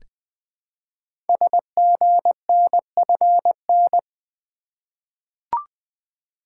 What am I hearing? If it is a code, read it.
Morse code, heard as a keyed tone, about 20 wpm, 700 Hz. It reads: SGNFN
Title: Code Groups - 5 Letters - 20wpm